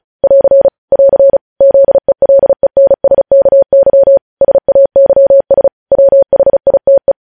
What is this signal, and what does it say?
Morse code, heard as a keyed tone, about 36 wpm, 550 Hz. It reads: <AR> <AR> ZELENSKY SAYS WHITE
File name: podcast / News headlines-1x-35x35